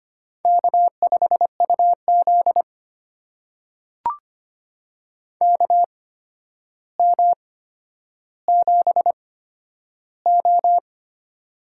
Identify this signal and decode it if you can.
Morse code, heard as a keyed tone, about 25 wpm, spaced as for 11 wpm, 700 Hz. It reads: K5UZ KM7O
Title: USA Call Signs - 4 Characters - 8x Character Spacing - 25wpm